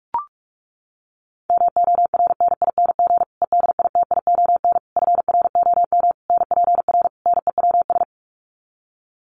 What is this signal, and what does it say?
Morse code, heard as a keyed tone, about 44 wpm, 700 Hz. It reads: MORNING EDITION FROM NPR NEWS
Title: Sets of 5 Words Encore 45wpm